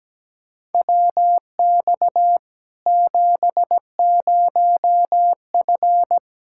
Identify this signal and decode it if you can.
Morse code, heard as a keyed tone, about 17 wpm, 700 Hz. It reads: WX 70F
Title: K1USN SST QSO Practice 17wpm